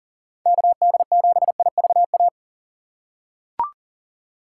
Morse code, heard as a keyed tone, about 40 wpm, 700 Hz. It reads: KD7IVA
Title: USA Call Signs - 6 Characters - 3x Character Spacing - 40wpm